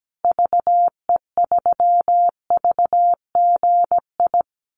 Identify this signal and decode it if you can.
Morse code, heard as a keyed tone, about 17 wpm, 700 Hz. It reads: VE3VGI
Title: Call Signs Repeated 2x - 17wpm